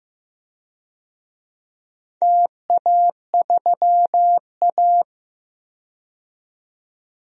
Morse code, heard as a keyed tone, about 15 wpm, 700 Hz. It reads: TA3A
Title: DXCC-100 Call Signs - 4 Characters - 15wpm